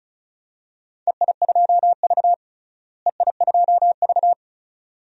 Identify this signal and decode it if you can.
Morse code, heard as a keyed tone, about 36 wpm, 700 Hz. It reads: EI2V EI2V
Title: DXCC-100 Call Signs - 4 Characters - Repeated 2x - 35wpm